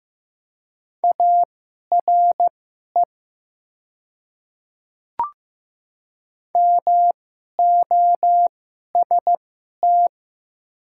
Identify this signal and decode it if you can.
Morse code, heard as a keyed tone, about 15 wpm, spaced as for 11 wpm, 700 Hz. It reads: ARE MOST
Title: 2x Character Spacing - Top 300 Words - 15wpm